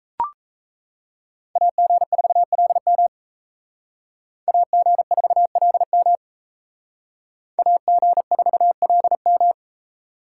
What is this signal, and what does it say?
Morse code, heard as a keyed tone, about 36 wpm, 700 Hz. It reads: AG4LM AG4LM AG4LM